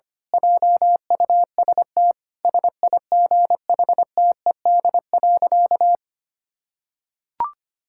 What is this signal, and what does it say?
Morse code, heard as a keyed tone, about 25 wpm, 700 Hz. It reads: JUST SIGHTED.
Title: Pangram Challenge v2 - 25wpm